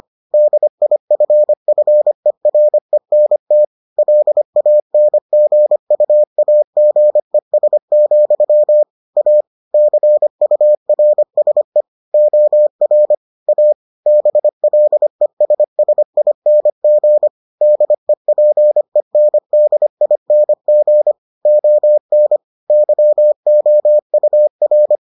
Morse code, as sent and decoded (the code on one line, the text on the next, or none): -.. .. ..-. ..-. .-. -. - / .-.. .- -. --. ..- .- --. ... --..-- / .- / -.-. ..- .-. ... / --- .-. / .- / -... .-.. ... ... .. -. --. / -.. .--. -. -.. .. -. --. / --- -. / -.-- --- ..- .-.
DIFFERENT LANGUAGES, A CURSE OR A BLESSING DEPENDING ON YOUR